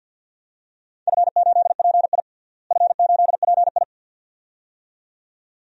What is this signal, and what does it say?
Morse code, heard as a keyed tone, about 47 wpm, 700 Hz. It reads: F8PI F8PI
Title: DXCC-100 Call Signs - 4 Characters - Repeated 2x - 50wpm